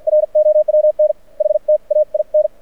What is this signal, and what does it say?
Morse code, heard as a keyed tone, about 52 wpm, spaced as for 43 wpm, 590 Hz. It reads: ROWN HTAIN
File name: archived / CRSnet